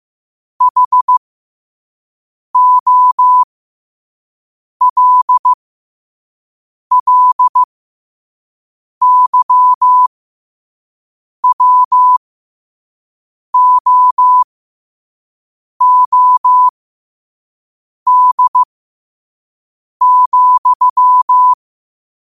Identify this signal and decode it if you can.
Morse code, heard as a keyed tone, about 15 wpm, spaced as for 5 wpm, 1000 Hz. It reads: HOLLYWOOD,
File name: qotdfiles / 5wpm